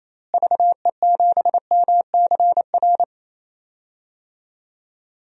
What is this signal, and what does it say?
Morse code, heard as a keyed tone, about 28 wpm, 700 Hz. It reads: VE7MCR
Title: DXCC-100 Call Signs - 6 Characters - Repeated 3x - 28wpm